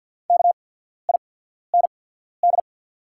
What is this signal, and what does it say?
Morse code, heard as a keyed tone, about 48 wpm, spaced as for 14 wpm, 690 Hz. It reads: KIND
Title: 8x Character Spacing - Top 300 Words - 50wpm